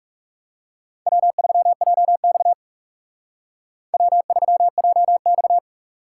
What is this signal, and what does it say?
Morse code, heard as a keyed tone, about 34 wpm, 700 Hz. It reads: W3JX W3JX
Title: Speed Racing - USA 1x2 Call Signs 30wpm